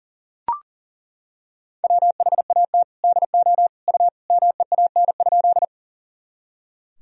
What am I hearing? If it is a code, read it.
Morse code, heard as a keyed tone, about 40 wpm, 700 Hz. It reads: WHAT DO U MEAN?
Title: Sentences from Top 200 Words and Abbreviations 40wpm